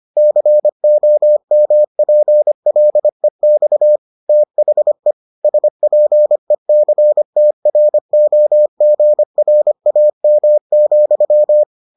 Morse code, heard as a keyed tone, about 25 wpm, 600 Hz. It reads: COMPLEX THE SPECTROGRAM,